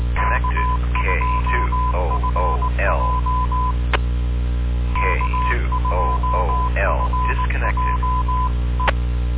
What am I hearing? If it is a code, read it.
Morse code, heard as a keyed tone, about 18 wpm, 1010 Hz, with noise. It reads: K2FJ K2FJ E